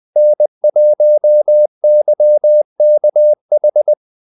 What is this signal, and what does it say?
Morse code, heard as a keyed tone, about 20 wpm, 600 Hz. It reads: N1YKH
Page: Kaldesignaler 20 wpm